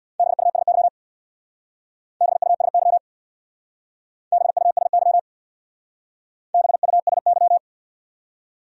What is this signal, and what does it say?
Morse code, heard as a keyed tone, about 66 wpm, 700 Hz. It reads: BUSY BUSY BUSY BUSY